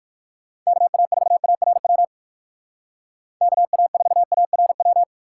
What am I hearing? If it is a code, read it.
Morse code, heard as a keyed tone, about 54 wpm, 700 Hz. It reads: KA4ARW KA4ARW
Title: Speed Racing - USA 2x3 Call Signs 45wpm